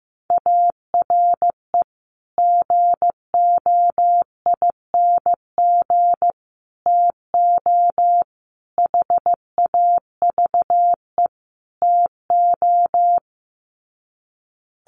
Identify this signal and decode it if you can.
Morse code, heard as a keyed tone, about 15 wpm, 700 Hz. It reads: ARE GOING TO HAVE TO
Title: Sets of 5 Words 15wpm